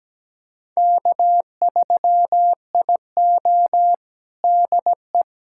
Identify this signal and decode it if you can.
Morse code, heard as a keyed tone, about 17 wpm, 700 Hz. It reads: K3IO DE
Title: USA Callsign DE Callsign - 4 Characters - 17wpm